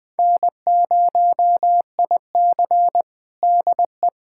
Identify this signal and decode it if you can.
Morse code, heard as a keyed tone, about 20 wpm, 700 Hz. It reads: N0IC DE
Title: USA Callsign DE Callsign - 4 Characters - 20wpm